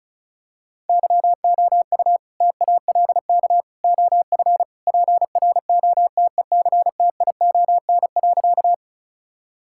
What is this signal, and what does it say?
Morse code, heard as a keyed tone, about 34 wpm, 700 Hz. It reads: YOU TALK OF PROTECTION.
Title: Sentences from Top 1600 Words 35wpm